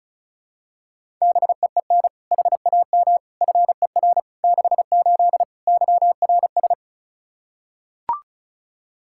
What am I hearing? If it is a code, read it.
Morse code, heard as a keyed tone, about 36 wpm, 700 Hz. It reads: BEEN HAM FER 68 YRS
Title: Practice QSO - Exchange 3 - More Info - 35wpm